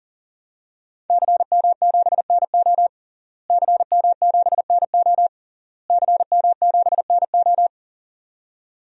Morse code, heard as a keyed tone, about 40 wpm, 700 Hz. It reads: CM7NO CM7NO CM7NO
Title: DXCC-100 Call Signs - 5 Characters - Repeated 3x - 40wpm